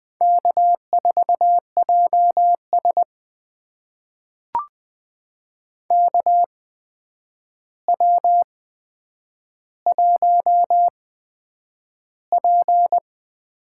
Morse code, heard as a keyed tone, about 20 wpm, spaced as for 9 wpm, 700 Hz. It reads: K4JS KW1P
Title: USA Call Signs - 4 Characters - 8x Character Spacing - 20wpm